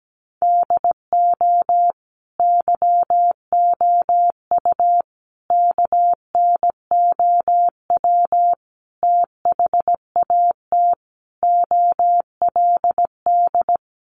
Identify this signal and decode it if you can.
Morse code, heard as a keyed tone, about 17 wpm, 700 Hz. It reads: DO YOU KNOW THAT OLD
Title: Sentences from Top 400 Words 17wpm